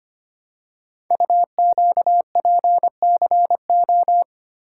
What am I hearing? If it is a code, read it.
Morse code, heard as a keyed tone, about 25 wpm, 700 Hz. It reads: UQPCO
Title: Code Groups - 5 Letters - 25wpm